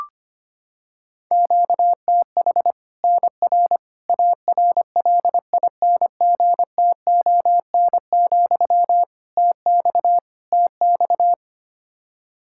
Morse code, heard as a keyed tone, about 25 wpm, 700 Hz. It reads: QTH NR ARLINGTON, TX TX
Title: Practice QSO - Exchange 1 - Beginning - 25wpm